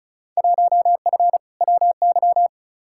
Morse code, heard as a keyed tone, about 36 wpm, 700 Hz. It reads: 1F WY
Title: Field Day Exchanges Encore - 35wpm